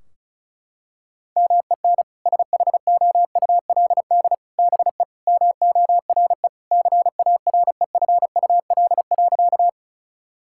Morse code, heard as a keyed tone, about 34 wpm, 700 Hz. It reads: MEN SHOULD BE MORE CAREFUL.
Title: Sentences from Top 1600 Words 35wpm